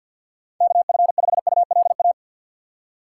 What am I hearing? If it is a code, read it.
Morse code, heard as a keyed tone, about 52 wpm, 700 Hz. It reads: KF5ULA
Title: USA Call Signs - 4x Character Spacing - 50wpm